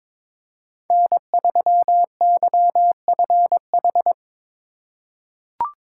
Morse code, heard as a keyed tone, about 22 wpm, 700 Hz. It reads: N3YFH
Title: USA Call Signs - 5 Characters - 4x Character Spacing - 22wpm